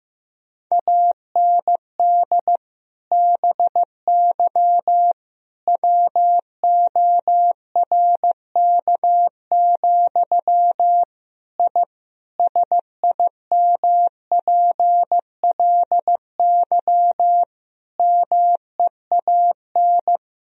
Morse code, heard as a keyed tone, about 15 wpm, 700 Hz. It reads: AND BY WORK, I SIMPLY MEAN
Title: Sentences from Top 1000 Words - Review Part 4 - 15wpm